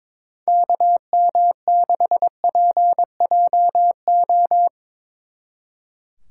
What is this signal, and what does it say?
Morse code, heard as a keyed tone, about 23 wpm, 700 Hz. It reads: KM6PJO
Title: Speed Racing - USA Call Signs 22wpm